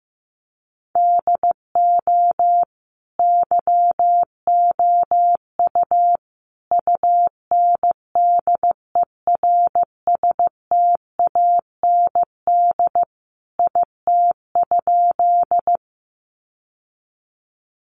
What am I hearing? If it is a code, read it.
Morse code, heard as a keyed tone, about 15 wpm, 700 Hz. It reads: DO YOU UNDERSTAND IT?
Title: Sentences from Top 600 Words 15wpm